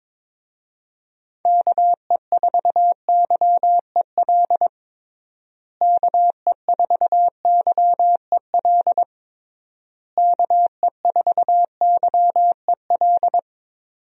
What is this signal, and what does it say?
Morse code, heard as a keyed tone, about 22 wpm, 700 Hz. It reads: KE4YEL KE4YEL KE4YEL
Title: USA Call Signs - 6 Characters - Repeated 3x - 22wpm